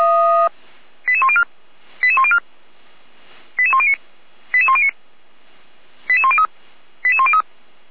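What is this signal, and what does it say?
Five-tone selective call (ZVEI), heard as quick runs of five tones, tones of about 70 ms. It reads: 88104 88104 88109 88109 88103 88103
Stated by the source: ZVEI five-tone selective call received by SDR.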